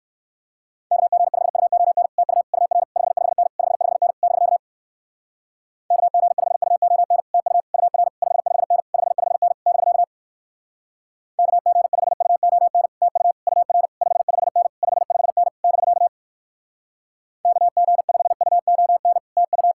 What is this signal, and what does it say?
Morse code, heard as a keyed tone, about 61 wpm, 700 Hz. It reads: KG5UON TU UR 55N 55N <BK> KG5UON TU UR 55N 55N <BK> KG5UON TU UR 55N 55N <BK> KG5UON TU